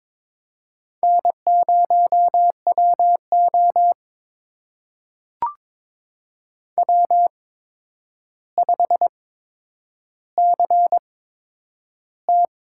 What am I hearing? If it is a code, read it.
Morse code, heard as a keyed tone, about 22 wpm, spaced as for 9 wpm, 700 Hz. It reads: N0WO W5CT
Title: USA Call Signs - 4 Characters - 8x Character Spacing - 22wpm